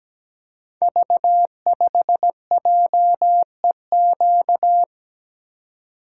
Morse code, heard as a keyed tone, about 17 wpm, 700 Hz. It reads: V5JEQ
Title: Code Groups - 5 Letter:Numbers - 17wpm